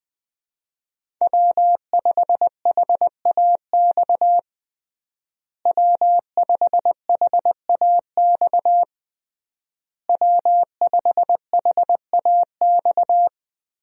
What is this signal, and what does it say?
Morse code, heard as a keyed tone, about 20 wpm, 700 Hz. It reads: W5HAX W5HAX W5HAX
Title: USA Call Signs - 5 Characters - Repeated 3x - 20wpm